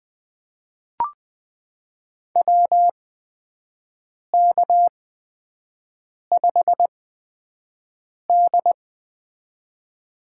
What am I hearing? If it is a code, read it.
Morse code, heard as a keyed tone, about 20 wpm, spaced as for 5 wpm, 700 Hz. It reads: WK5D
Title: USA Call Signs - 4 Characters - 8x Character Spacing - 20wpm